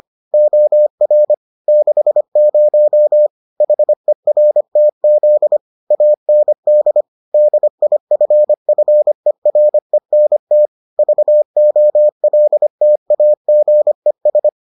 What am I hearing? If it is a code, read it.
Morse code, heard as a keyed tone, about 25 wpm, 600 Hz. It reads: OR 60 HERTZ AND DIFFERENT VOLTAGES